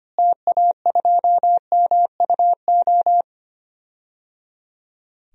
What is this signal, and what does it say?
Morse code, heard as a keyed tone, about 25 wpm, 700 Hz. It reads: TA2MUO
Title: DXCC-100 Call Signs - 6 Characters - Repeated 3x - 25wpm